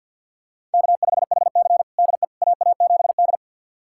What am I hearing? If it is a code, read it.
Morse code, heard as a keyed tone, about 51 wpm, 700 Hz. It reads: K5HC DE AA7D
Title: USA Callsign DE Callsign - 4 Characters - 50wpm